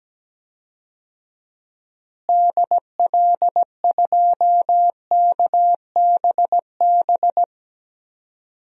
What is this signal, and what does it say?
Morse code, heard as a keyed tone, about 17 wpm, 700 Hz. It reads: DL2KBB
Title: DXCC-100 Call Signs - 6 Characters - Repeated 3x - 17wpm